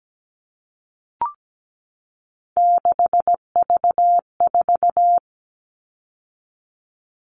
Morse code, heard as a keyed tone, about 17 wpm, 700 Hz. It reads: 6V4
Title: All Cut Numbers - 3-Digit 17wpm